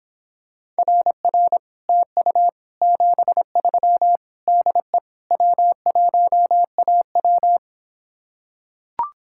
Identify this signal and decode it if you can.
Morse code, heard as a keyed tone, about 26 wpm, 700 Hz. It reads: RR TU 73 DE W1AW